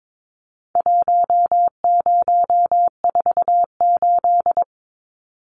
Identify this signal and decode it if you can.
Morse code, heard as a keyed tone, about 22 wpm, 700 Hz. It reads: 1048
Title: Numbers - Contest Style - 4-Digit 22wpm